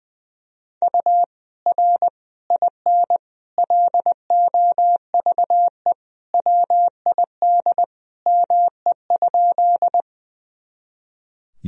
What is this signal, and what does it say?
Morse code, heard as a keyed tone, about 20 wpm, 700 Hz. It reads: U R IN LOVE WID ME?
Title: Sentences from Top 400 Words and Abbreviations 20wpm